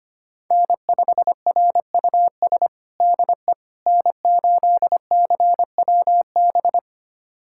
Morse code, heard as a keyed tone, about 25 wpm, 700 Hz. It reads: N5RUS DE N8CWB
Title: USA Callsign DE Callsign - 5 Characters - 25wpm